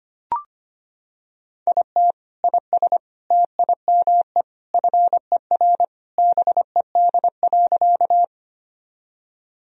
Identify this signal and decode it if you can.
Morse code, heard as a keyed tone, about 25 wpm, 700 Hz. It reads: IT IS TIME FER BED.
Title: Sentences from Top 500 Words and Abbreviations 25wpm